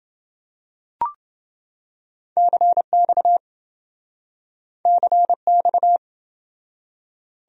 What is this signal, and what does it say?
Morse code, heard as a keyed tone, about 27 wpm, 700 Hz. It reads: CX CX